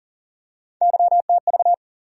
Morse code, heard as a keyed tone, about 40 wpm, 700 Hz. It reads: YTV
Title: USA Call Sign Postfixes - 3 Letters 40wpm